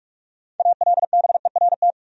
Morse code, heard as a keyed tone, about 46 wpm, 700 Hz. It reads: ALBERT
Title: USA Common Names - 45wpm